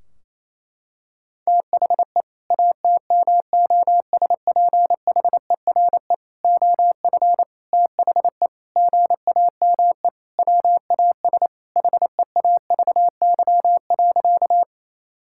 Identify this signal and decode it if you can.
Morse code, heard as a keyed tone, about 27 wpm, 700 Hz. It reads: THE ATMOSPHERE OF THE GAME WAS HEAVY.
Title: Sentences from Top 1600 Words 28wpm